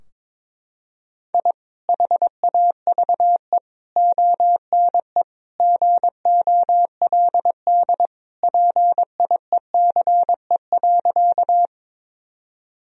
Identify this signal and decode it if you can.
Morse code, heard as a keyed tone, about 22 wpm, 700 Hz. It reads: I HAVE ONE GOLD PIECE.
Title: Sentences from Top 1000 Words - Review Part 3 - 22wpm